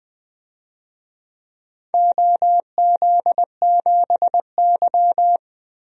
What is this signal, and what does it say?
Morse code, heard as a keyed tone, about 20 wpm, 700 Hz. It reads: OZ7Y
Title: DXCC-100 Call Signs - 4 Characters - 20wpm